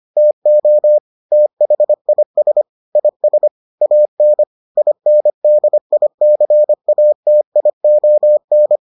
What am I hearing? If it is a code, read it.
Morse code, heard as a keyed tone, about 25 wpm, 600 Hz. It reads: TO THIS IS AN INDICATION